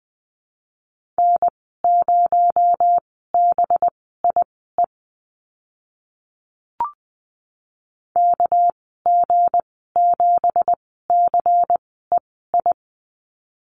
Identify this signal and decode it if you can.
Morse code, heard as a keyed tone, about 20 wpm, spaced as for 14 wpm, 700 Hz. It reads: N0BIE KG7CEI
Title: 238 Lesson 38g - Letter q - Call Signs 20wpm